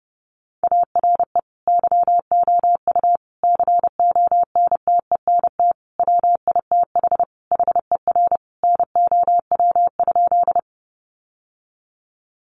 Morse code, heard as a keyed tone, about 30 wpm, 700 Hz. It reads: ARE YOU CONTENT WITH HER NOW?
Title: Sentences from Top 900 Words 30wpm